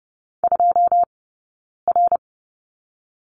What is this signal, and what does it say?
Morse code, heard as a keyed tone, about 30 wpm, 700 Hz. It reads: 2 R
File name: course / 30:10wpm / 086 Lesson 15d - Number 5 - Two Characters 30wpm